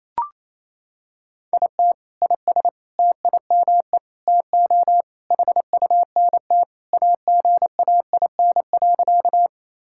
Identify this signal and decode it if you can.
Morse code, heard as a keyed tone, about 28 wpm, 700 Hz. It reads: IT IS TIME TO HUNT AGAIN.
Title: Sentences from Top 1600 Words 28wpm